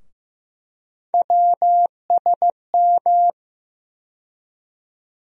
Morse code, heard as a keyed tone, about 15 wpm, 700 Hz. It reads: WSM
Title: USA Call Sign Postfixes - 3 Letters 15wpm